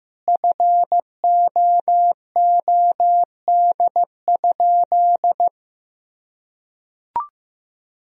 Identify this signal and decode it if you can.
Morse code, heard as a keyed tone, about 15 wpm, 700 Hz. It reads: FOOD?
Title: Sentences from Top 1000 Words - Review Part 2 - 15wpm